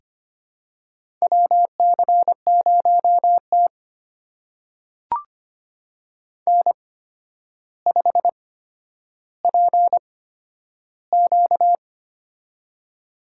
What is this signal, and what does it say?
Morse code, heard as a keyed tone, about 25 wpm, spaced as for 11 wpm, 700 Hz. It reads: WC0T N5PQ
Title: USA Call Signs - 4 Characters - 8x Character Spacing - 25wpm